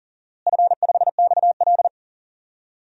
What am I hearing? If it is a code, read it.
Morse code, heard as a keyed tone, about 37 wpm, 700 Hz. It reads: F5XL
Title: DXCC-100 Call Signs - 4 Characters - Repeated 3x - 40wpm